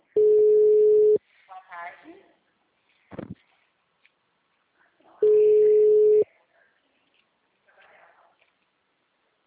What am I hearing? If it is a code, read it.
Ringback tone (Europe, 425 Hz), with speech.